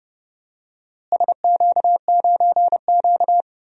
Morse code, heard as a keyed tone, about 30 wpm, 700 Hz. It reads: SQ9Q
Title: DXCC-100 Call Signs - 4 Characters - Repeated 3x - 30wpm